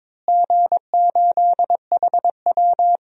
Morse code, heard as a keyed tone, about 21 wpm, 700 Hz. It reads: G8HW
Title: DXCC-100 Call Signs - 4 Characters - Repeated 2x - 22wpm